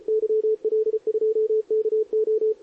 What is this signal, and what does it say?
Morse code, heard as a keyed tone, about 33 wpm, 440 Hz, with radio noise. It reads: YL2KO
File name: CQWWCW2023 / SP